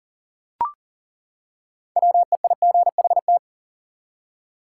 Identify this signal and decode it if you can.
Morse code, heard as a keyed tone, about 39 wpm, 700 Hz. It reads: WEIGHT
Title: Six Letter Words - 40wpm